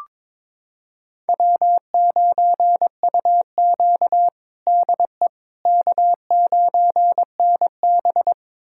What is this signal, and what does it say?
Morse code, heard as a keyed tone, about 22 wpm, 700 Hz. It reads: W9UQ DE K9NB
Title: USA Callsign DE Callsign - 4 Characters - 22wpm